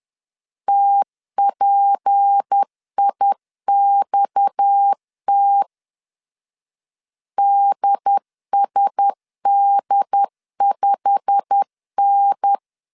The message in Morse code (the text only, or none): TPIXT DSD5N